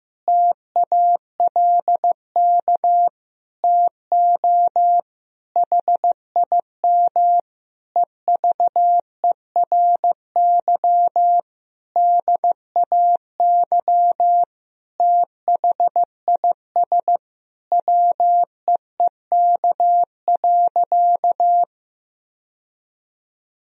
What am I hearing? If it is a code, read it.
Morse code, heard as a keyed tone, about 15 wpm, 700 Hz. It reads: TALK TO HIM EVERY DAY THIS WEEK.
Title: Sentences from Top 1000 Words - Review Part 2 - 15wpm